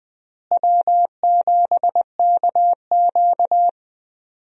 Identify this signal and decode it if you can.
Morse code, heard as a keyed tone, about 20 wpm, 700 Hz. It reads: W7KQ
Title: USA Call Signs - 4 Characters - Repeated 3x - 20wpm